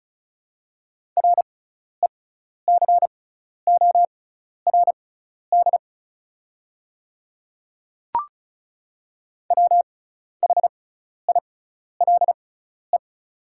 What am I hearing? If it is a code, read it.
Morse code, heard as a keyed tone, about 36 wpm, spaced as for 12 wpm, 690 Hz. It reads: RECORD WHILE
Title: 6x Character Spacing - Top 500 Words - 35wpm